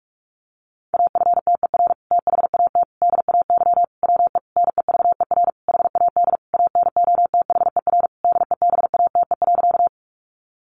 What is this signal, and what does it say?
Morse code, heard as a keyed tone, about 44 wpm, 700 Hz. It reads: AFTER THAT DAY WE NEVER HAD ANOTHER DEBATE.
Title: Sentences from Top 1100 Words 45wpm